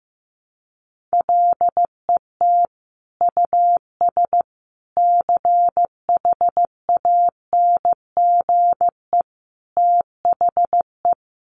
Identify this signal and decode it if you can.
Morse code, heard as a keyed tone, about 15 wpm, 700 Hz. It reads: LET US CHANGE THE
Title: Sentences from Top 800 Words 15wpm